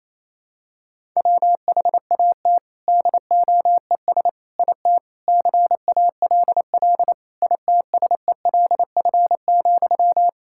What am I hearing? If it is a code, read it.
Morse code, heard as a keyed tone, about 28 wpm, 700 Hz. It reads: WHAT DOES IT CALL ITSELF,
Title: Sentences from Top 1000 Words - Review Part 5 - 28wpm